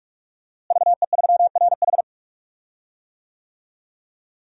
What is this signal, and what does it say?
Morse code, heard as a keyed tone, about 46 wpm, 700 Hz. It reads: VE3RH
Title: DXCC-100 Call Signs - 5 Characters - Repeated 2x - 45wpm